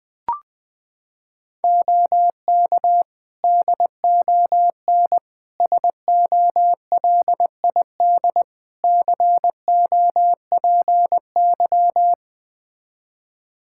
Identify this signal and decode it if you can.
Morse code, heard as a keyed tone, about 20 wpm, 700 Hz. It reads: OK DON SOLID COPY
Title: Practice QSO - Exchange 3 - More Info - 20wpm